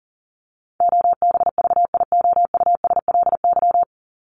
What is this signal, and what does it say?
Morse code, heard as a keyed tone, about 40 wpm, 700 Hz. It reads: OBVIOUSLY
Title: Top 1501-1750 Words 40wpm